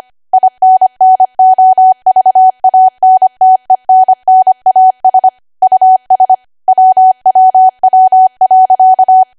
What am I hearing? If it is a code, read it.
Morse code, heard as a keyed tone, about 25 wpm, 750 Hz. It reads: INNOVANTENNAS US WWW.
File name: Archive / 25 WPM / 25WPM